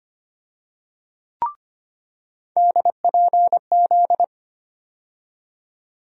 Morse code, heard as a keyed tone, about 25 wpm, 700 Hz. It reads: DPZ
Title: USA Call Sign Postfixes - 3 Letters 25wpm